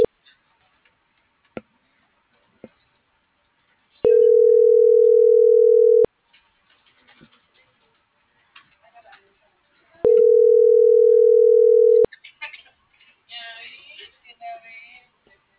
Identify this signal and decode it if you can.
Ringback tone (North America), 440 + 480 Hz.